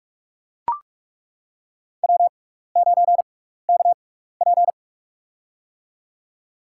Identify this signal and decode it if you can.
Morse code, heard as a keyed tone, about 42 wpm, spaced as for 15 wpm, 700 Hz. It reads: W9KP
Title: USA Call Signs - 4 Characters - 6x Character Spacing - 45wpm